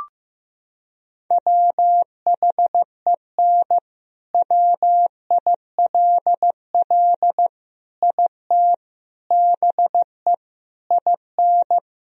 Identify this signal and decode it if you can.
Morse code, heard as a keyed tone, about 15 wpm, 700 Hz. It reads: WHEN WILL IT BE IN
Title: Sentences from Top 1000 Words - Review Part 4 - 15wpm